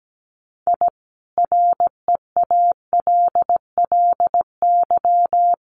Morse code, heard as a keyed tone, about 17 wpm, 700 Hz. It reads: I REALLY
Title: Sets of 5 Words Encore 17wpm